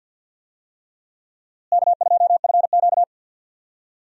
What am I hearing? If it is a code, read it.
Morse code, heard as a keyed tone, about 52 wpm, 700 Hz. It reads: K2FQ
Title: USA Call Signs - 4 Characters - Repeated 3x - 50wpm